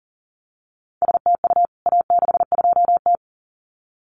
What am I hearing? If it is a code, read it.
Morse code, heard as a keyed tone, about 40 wpm, 700 Hz. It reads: STU A62T
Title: CWT Exchange 40wpm